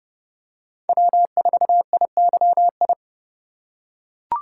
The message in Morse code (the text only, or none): W4IYI